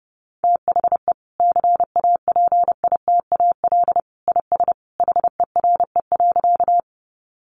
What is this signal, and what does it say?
Morse code, heard as a keyed tone, about 30 wpm, 700 Hz. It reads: THE CAPITAL IS HERE.
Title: Sentences from Top 1000 Words 30wpm